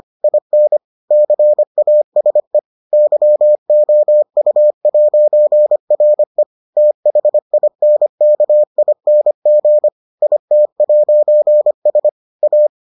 Morse code, heard as a keyed tone, about 25 wpm, 600 Hz. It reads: IN CASE YOU'RE THINKING IT'S A